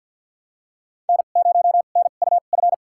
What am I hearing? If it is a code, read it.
Morse code, heard as a keyed tone, about 52 wpm, spaced as for 37 wpm, 700 Hz. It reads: N0NUF
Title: USA Call Signs - 5 Characters - 2x Character Spacing - 50wpm